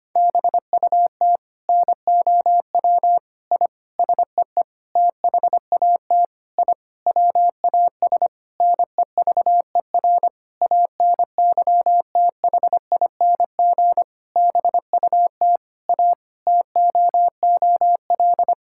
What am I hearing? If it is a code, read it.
Morse code, heard as a keyed tone, about 25 wpm, 700 Hz. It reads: BUT NOW I SEE THAT I WAS NEVER ANYTHING BUT A TOOL
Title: Sentences from Top 1000 Words - Review Part 6 - 25wpm